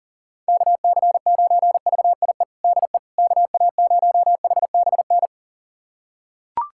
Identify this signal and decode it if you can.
Morse code, heard as a keyed tone, about 37 wpm, 700 Hz. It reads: KC9VIE DE KA0HBN
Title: USA Callsign DE Callsign - 6 Characters - 40wpm